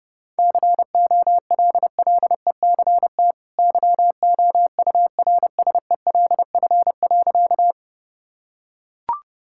Morse code, heard as a keyed tone, about 32 wpm, 700 Hz. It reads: COLLECT YOURSELF.
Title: Sentences from Top 1600 Words 30wpm